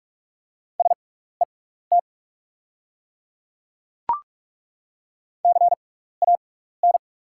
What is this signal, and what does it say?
Morse code, heard as a keyed tone, about 42 wpm, spaced as for 15 wpm, 700 Hz. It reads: SET CAN
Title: 6x Character Spacing - Top 500 Words - 45wpm